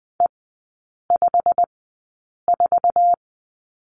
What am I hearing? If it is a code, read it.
Morse code, heard as a keyed tone, about 20 wpm, spaced as for 8 wpm, 700 Hz. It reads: E54
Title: 087 Lesson 15e - Number 5 - Three Characters 20wpm